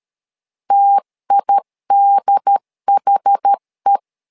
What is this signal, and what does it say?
Morse code, heard as a keyed tone, about 13 wpm, 800 Hz. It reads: TIDHE
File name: LEZIONI / 2 UHDN / 2 UHD 12wpm 1540